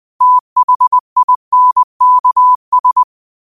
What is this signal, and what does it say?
Morse code, heard as a keyed tone, about 20 wpm, 1000 Hz. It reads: THINKS
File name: qotdfiles / 20wpm